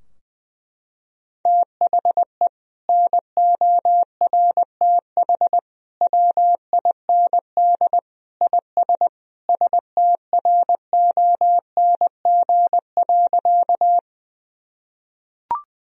Morse code, heard as a keyed tone, about 20 wpm, 700 Hz. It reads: THE NORTH WIND IS STRONG.
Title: Sentences from Top 1000 Words - Review Part 2 - 20wpm